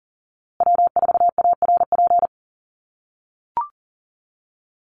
Morse code, heard as a keyed tone, about 40 wpm, 700 Hz. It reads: W4ARP
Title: USA Call Signs Repeated 3x 40wpm